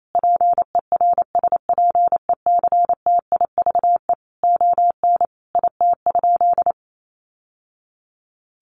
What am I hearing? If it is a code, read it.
Morse code, heard as a keyed tone, about 28 wpm, 700 Hz. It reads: PERSPECTIVE ON IT?
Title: Sentences from Top 1200 Words 28wpm